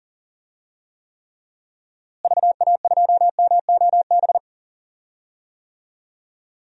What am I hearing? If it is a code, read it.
Morse code, heard as a keyed tone, about 40 wpm, 700 Hz. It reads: VA2MOB
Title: DXCC-100 Call Signs - 6 Characters - Repeated 3x - 40wpm